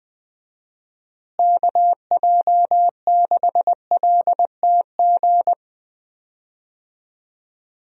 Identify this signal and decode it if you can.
Morse code, heard as a keyed tone, about 20 wpm, 700 Hz. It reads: KJ6LTG
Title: USA Call Signs - 6 Characters - Repeated 2x - 20wpm